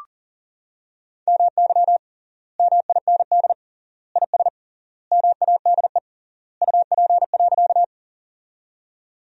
Morse code, heard as a keyed tone, about 40 wpm, 700 Hz. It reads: MY MIND IS MADE UP.
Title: 3x Word-Spacing - Sentences from Top 500 Words - 40wpm